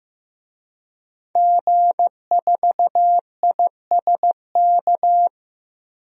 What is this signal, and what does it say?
Morse code, heard as a keyed tone, about 15 wpm, 700 Hz. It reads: G4ISK
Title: DXCC-100 Call Signs - 5 Characters - 15wpm